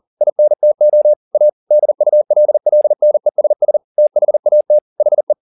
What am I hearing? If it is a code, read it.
Morse code, heard as a keyed tone, about 40 wpm, 600 Hz. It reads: INTO A DULLNESS THAT HE
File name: the old man and the sea 2 40wpm